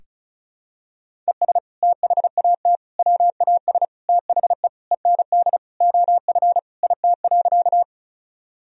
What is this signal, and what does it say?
Morse code, heard as a keyed tone, about 36 wpm, 700 Hz. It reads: ES THAT WAS THE END OF IT.
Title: Sentences from Top 200 Words and Abbreviations 35wpm